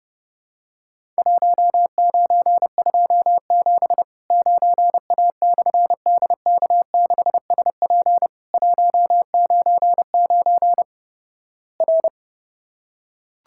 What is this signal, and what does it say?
Morse code, heard as a keyed tone, about 31 wpm, 700 Hz. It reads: 1927 9A/DK6SP 199 R
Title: Worked All Europe DX Contest - QTC Exchange - 30wpm